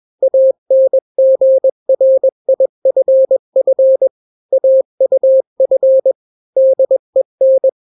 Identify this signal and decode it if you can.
Morse code, heard as a keyed tone, about 21 wpm, 530 Hz. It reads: ANGRIFF AUF DEN
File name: daily / 20wpm